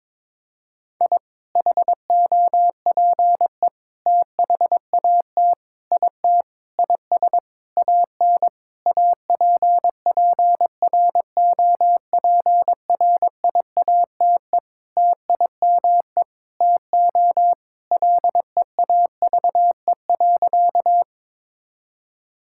Morse code, heard as a keyed tone, about 22 wpm, 700 Hz. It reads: I HOPE THAT IT IS AN APPROPRIATE TIME TO LEAVE.
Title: Sentences from Top 1000 Words - Review Part 5 - 22wpm